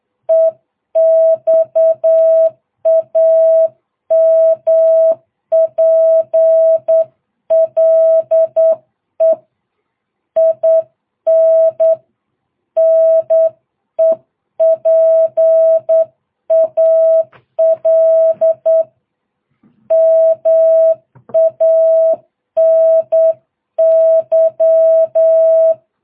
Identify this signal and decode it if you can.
Morse code, heard as a keyed tone, about 8 wpm, 660 Hz. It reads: EXAMPLE IN NEPAL MANY